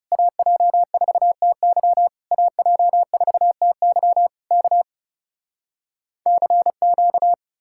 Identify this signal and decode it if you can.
Morse code, heard as a keyed tone, about 40 wpm, 700 Hz. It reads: AJ4TY AJ4TY K CQ